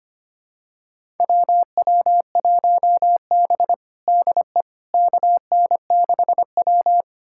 Morse code, heard as a keyed tone, about 25 wpm, 700 Hz. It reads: WW1B DE KN6W
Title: USA Callsign DE Callsign - 4 Characters - 25wpm